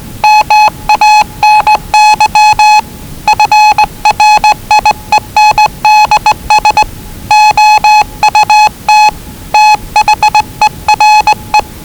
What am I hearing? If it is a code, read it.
Morse code, heard as a keyed tone, about 20 wpm, 840 Hz. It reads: MANY FRIENDS OUT THERE
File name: WellingtonRadioZlwCloseDownOnHfAndMf / 01-Zlw 4277khz 417.5 qrt until 0925z